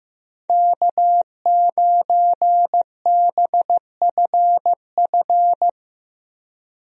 Morse code, heard as a keyed tone, about 15 wpm, 700 Hz. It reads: K9BFF
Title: USA Call Signs - 5 Characters - Repeated 2x - 15wpm